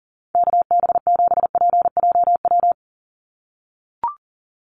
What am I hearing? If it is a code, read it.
Morse code, heard as a keyed tone, about 40 wpm, 700 Hz. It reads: KB7PJW
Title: USA Call Signs 40wpm